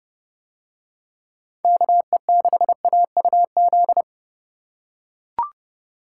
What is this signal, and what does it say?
Morse code, heard as a keyed tone, about 30 wpm, 700 Hz. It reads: KE6AUZ